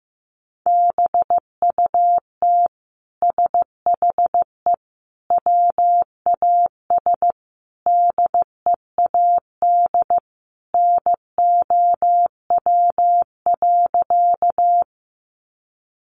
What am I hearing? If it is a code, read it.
Morse code, heard as a keyed tone, about 15 wpm, 700 Hz. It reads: BUT SHE WAS DEAD NOW.
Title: Sentences from Top 1500 Words 15wpm